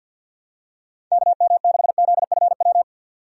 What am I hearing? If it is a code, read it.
Morse code, heard as a keyed tone, about 51 wpm, 700 Hz. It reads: KM6ZFW
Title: USA Call Signs - 6 Characters - 50wpm